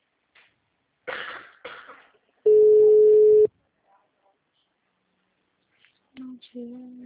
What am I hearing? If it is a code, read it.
Ringback tone (Europe, 425 Hz).